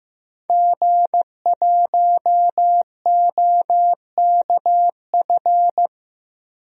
Morse code, heard as a keyed tone, about 15 wpm, 700 Hz. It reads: G1OKF
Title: DXCC-100 Call Signs - 5 Characters - 15wpm